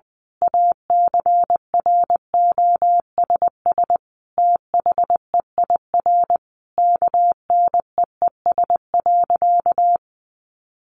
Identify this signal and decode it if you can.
Morse code, heard as a keyed tone, about 20 wpm, 700 Hz. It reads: ACROSS THEIR KNEES.
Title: A Sound Of Thunder 20wpm